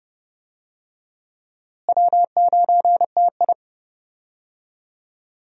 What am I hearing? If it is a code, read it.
Morse code, heard as a keyed tone, about 30 wpm, 700 Hz. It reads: W9TI
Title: USA Call Signs - 4 Characters - Repeated 3x - 30wpm